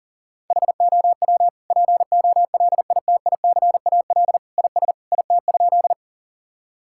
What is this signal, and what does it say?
Morse code, heard as a keyed tone, about 37 wpm, 700 Hz. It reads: HOW POLITICAL IS IT?
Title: Sentences from Top 1000 Words - Review Part 5 - 40wpm